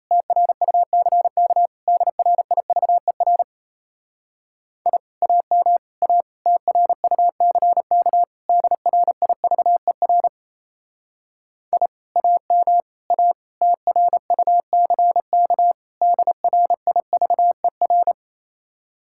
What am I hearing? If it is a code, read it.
Morse code, heard as a keyed tone, about 36 wpm, 700 Hz. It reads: TRUCK DRIVER I AM A TRUCK DRIVER I AM A TRUCK DRIVER